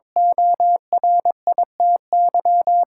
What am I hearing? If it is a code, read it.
Morse code, heard as a keyed tone, about 22 wpm, 700 Hz. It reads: ORITY
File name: practice / 22wpm / Eight Letter Words - 22wpm